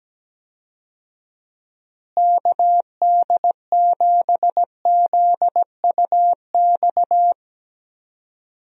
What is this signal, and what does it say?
Morse code, heard as a keyed tone, about 17 wpm, 700 Hz. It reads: KD7ZUX
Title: USA Call Signs - 6 Characters - Repeated 2x - 17wpm